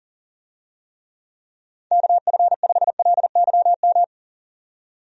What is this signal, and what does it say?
Morse code, heard as a keyed tone, about 37 wpm, 700 Hz. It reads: KF5LYM
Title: USA Call Signs - 6 Characters - 40wpm